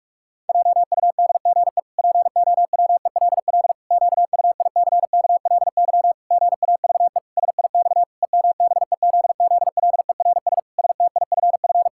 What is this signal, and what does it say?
Morse code, heard as a keyed tone, about 46 wpm, 700 Hz. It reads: JUDGE POWELL QUICKLY GAVE SIX EMBEZZLERS STIFF
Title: Pangram Challenge v2 - 45wpm